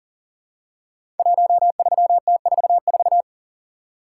Morse code, heard as a keyed tone, about 38 wpm, 700 Hz. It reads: 13T44
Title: Code Groups - 5 Letter:Numbers - 40wpm